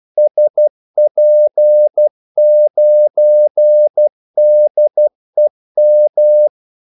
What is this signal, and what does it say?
Morse code, heard as a keyed tone, about 12 wpm, 600 Hz. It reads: SP9DEM
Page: 2 ekstra ordmellemrum mellem kaldesignalerne